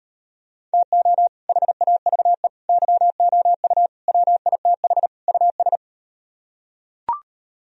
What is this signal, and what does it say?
Morse code, heard as a keyed tone, about 38 wpm, 700 Hz. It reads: TO HAVE YOU WITH US